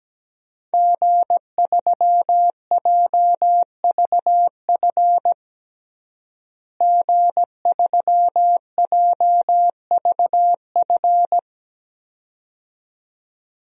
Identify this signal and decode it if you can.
Morse code, heard as a keyed tone, about 17 wpm, 700 Hz. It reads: G3JVF G3JVF
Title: DXCC-100 Call Signs - 5 Characters - Repeated 2x - 17wpm